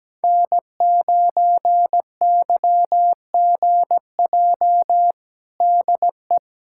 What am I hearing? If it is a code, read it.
Morse code, heard as a keyed tone, about 17 wpm, 700 Hz. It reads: N9YGJ DE
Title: USA Callsign DE Callsign 17wpm